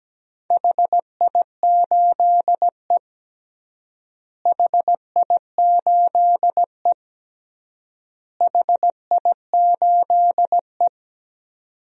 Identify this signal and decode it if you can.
Morse code, heard as a keyed tone, about 17 wpm, 700 Hz. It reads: HI8E HI8E HI8E
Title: DXCC-100 Call Signs - 4 Characters - Repeated 3x - 17wpm